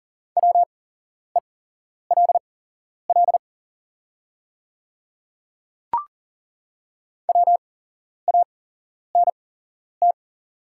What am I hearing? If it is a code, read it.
Morse code, heard as a keyed tone, about 40 wpm, spaced as for 11 wpm, 690 Hz. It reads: WELL WANT
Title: 8x Character Spacing - Top 200 Words - 40wpm